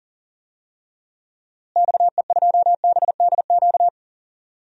Morse code, heard as a keyed tone, about 40 wpm, 700 Hz. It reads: XE2BDQ
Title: DXCC-100 Call Signs - 6 Characters - Repeated 2x - 40wpm